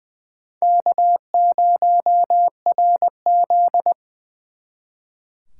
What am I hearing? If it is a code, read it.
Morse code, heard as a keyed tone, about 23 wpm, 700 Hz. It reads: K0RZ
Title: Speed Racing - USA 1x2 Call Signs 20wpm